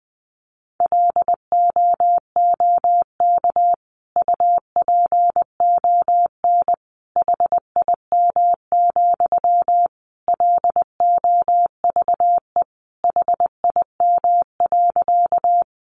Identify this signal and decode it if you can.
Morse code, heard as a keyed tone, about 20 wpm, 700 Hz. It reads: LOOK UPON HIM, LOVE HIM.
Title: Sentences from Top 1100 Words 20wpm